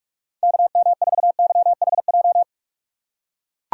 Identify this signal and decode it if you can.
Morse code, heard as a keyed tone, about 44 wpm, 700 Hz. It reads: KM4YHJ
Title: USA Call Signs - 6 Characters - 6x Character Spacing - 45wpm